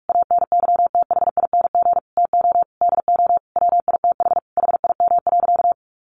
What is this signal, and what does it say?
Morse code, heard as a keyed tone, about 44 wpm, 700 Hz. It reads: ANYTHING TO DO WITH HIM.
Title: Sentences from Top 1000 Words 45wpm